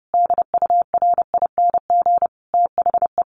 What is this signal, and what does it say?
Morse code, heard as a keyed tone, about 30 wpm, 700 Hz. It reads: DURING THE
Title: Sentences from Top 1000 Words 30wpm